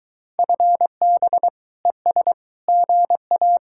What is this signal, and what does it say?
Morse code, heard as a keyed tone, about 23 wpm, 700 Hz. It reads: FB ES GA